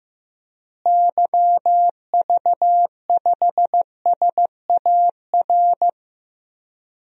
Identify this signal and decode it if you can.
Morse code, heard as a keyed tone, about 15 wpm, 700 Hz. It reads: YV5SAR
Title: DXCC-100 Call Signs - 6 Characters - Repeated 3x - 15wpm